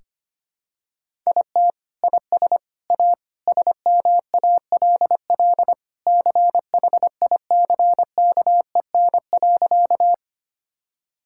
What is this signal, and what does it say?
Morse code, heard as a keyed tone, about 25 wpm, 700 Hz. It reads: IT IS A SMALL CHICKEN.
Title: Sentences from Top 1000 Words - Review Part 5 - 25wpm